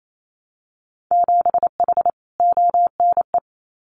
Morse code, heard as a keyed tone, about 28 wpm, 700 Hz. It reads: 7H ONE
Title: Winter Field Day Exchanges Encore 28wpm